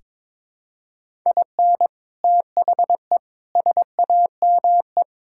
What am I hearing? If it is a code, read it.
Morse code, heard as a keyed tone, about 22 wpm, 700 Hz. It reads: IN THE SAME
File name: practice / 22wpm / Build A Phrase - 5-Word Phrases - 22wpm